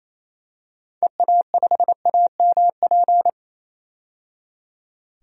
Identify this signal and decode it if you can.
Morse code, heard as a keyed tone, about 28 wpm, 700 Hz. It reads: EA5AMP
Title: DXCC-100 Call Signs - 6 Characters - 28wpm